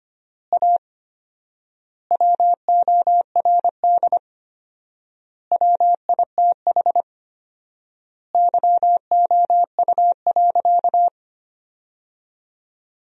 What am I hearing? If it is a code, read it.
Morse code, heard as a keyed tone, about 25 wpm, 700 Hz. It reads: A WORD WITH YOU.
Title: 4x Word-Spacing - Sentences from Top 300 Words - 25wpm